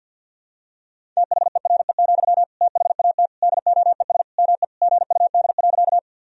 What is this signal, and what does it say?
Morse code, heard as a keyed tone, about 52 wpm, 700 Hz. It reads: THERE, THAT DOES ME GUD.
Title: Sentences from Top 200 Words and Abbreviations 50wpm